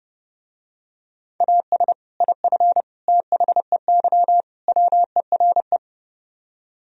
Morse code, heard as a keyed tone, about 30 wpm, 700 Hz. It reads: AS IF THEY WERE